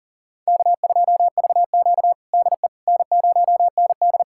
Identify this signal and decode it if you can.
Morse code, heard as a keyed tone, about 40 wpm, 700 Hz. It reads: K2VQ DE N0ND
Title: USA Callsign DE Callsign - 4 Characters - 40wpm